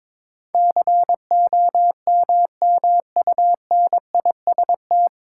Morse code, heard as a keyed tone, about 22 wpm, 700 Hz. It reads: COMMUNIST
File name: 9 Letter Words - First 3 Letters Given - 22wpm